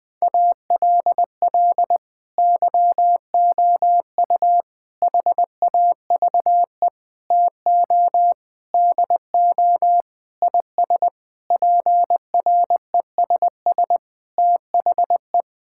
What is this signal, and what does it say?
Morse code, heard as a keyed tone, about 20 wpm, 700 Hz. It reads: ALL YOU HAVE TO DO IS PRESS THE
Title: Sentences with 6-Word Phrases - Spoken Beginning - 20wpm